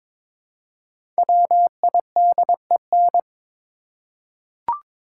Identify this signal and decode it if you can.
Morse code, heard as a keyed tone, about 22 wpm, 700 Hz. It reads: WIDEN
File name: 5 Letter Words - First 3 Letters Given - 22wpm